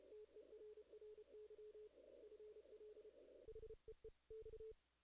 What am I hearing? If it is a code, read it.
Morse code, heard as a keyed tone, about 31 wpm, 440 Hz. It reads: TERRY FL HEE X
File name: 2016 JAN CW NAQP / 2016 Jan naqp k5ot - 0-8